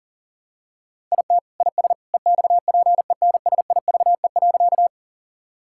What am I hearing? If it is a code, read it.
Morse code, heard as a keyed tone, about 40 wpm, 700 Hz. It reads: IT IS EXPENSIVE.
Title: Sentences from Top 1000 Words - Review Part 6 - 40wpm